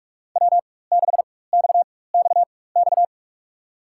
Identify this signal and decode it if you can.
Morse code, heard as a keyed tone, about 46 wpm, spaced as for 21 wpm, 700 Hz. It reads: W6XXX
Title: USA Call Signs - 5 Characters - 4x Character Spacing - 45wpm